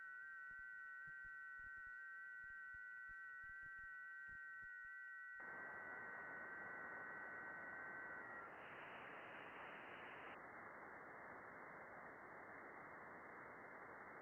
FT8.
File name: contests / june-vhf-k5tr- -0200